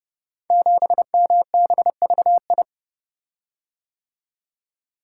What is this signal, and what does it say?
Morse code, heard as a keyed tone, about 30 wpm, 700 Hz. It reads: 7MBVI
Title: Code Groups - 5 Letter:Numbers - 30wpm